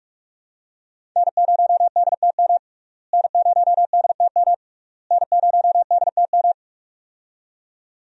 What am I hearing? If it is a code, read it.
Morse code, heard as a keyed tone, about 41 wpm, 700 Hz. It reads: N0DTM N0DTM N0DTM
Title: USA Call Signs - 5 Characters - Repeated 3x - 45wpm